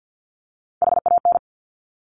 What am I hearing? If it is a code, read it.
Morse code, heard as a keyed tone, about 50 wpm, 700 Hz. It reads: HAN
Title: Top Three Letter Combinations in Words 50wpm